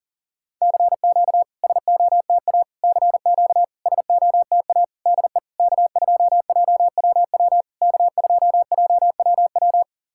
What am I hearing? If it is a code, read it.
Morse code, heard as a keyed tone, about 40 wpm, 690 Hz. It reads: CQ SOTA CQ SOTA DE K2JWW K2JWW
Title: SOTA QSO Practice 40wpm